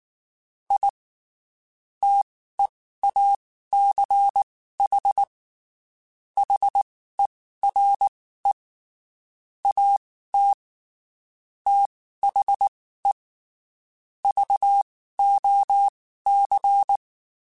Morse code, heard as a keyed tone, about 19 wpm, spaced as for 14 wpm, 800 Hz. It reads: I TEACH HERE AT THE VOC
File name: (spaced) / QSOs Spaced-mp3s / WD0BGP 1 @18 sp